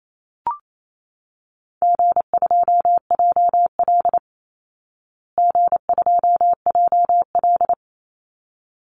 Morse code, heard as a keyed tone, about 28 wpm, 700 Hz. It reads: G2JL G2JL
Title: Easy Call Signs Repeated 2x - 28wpm